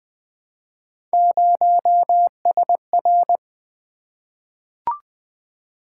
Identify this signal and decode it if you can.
Morse code, heard as a keyed tone, about 20 wpm, 700 Hz. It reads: 0SR
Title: Three Letter-Number Combinations - 20wpm